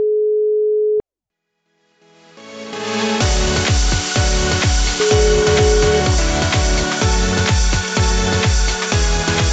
Ringback tone (Europe, 425 Hz), with noise.